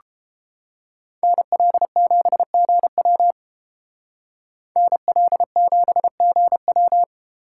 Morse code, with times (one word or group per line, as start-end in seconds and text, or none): 1.2-3.3 NL7GW
4.8-7.0 NL7GW